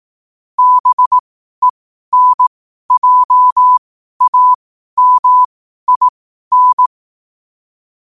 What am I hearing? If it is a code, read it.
Morse code, heard as a keyed tone, about 18 wpm, spaced as for 12 wpm, 1000 Hz. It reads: BENJAMIN
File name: qotdfiles / 12wpm